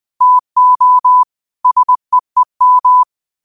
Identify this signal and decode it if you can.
Morse code, heard as a keyed tone, about 20 wpm, 1000 Hz. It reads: TO SEEM
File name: qotdfiles / 20wpm